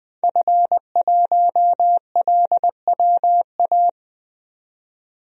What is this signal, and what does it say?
Morse code, heard as a keyed tone, about 20 wpm, 700 Hz. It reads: F1LWA
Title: DXCC-100 Call Signs - 5 Characters - 20wpm